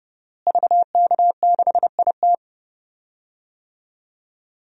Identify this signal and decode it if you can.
Morse code, heard as a keyed tone, about 32 wpm, 700 Hz. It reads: VK6IT
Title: DXCC-100 Call Signs - 5 Characters - Repeated 3x - 30wpm